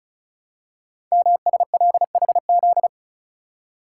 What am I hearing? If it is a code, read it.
Morse code, heard as a keyed tone, about 36 wpm, 700 Hz. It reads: MSLHZ
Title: Code Groups - 5 Letter:Numbers - 35wpm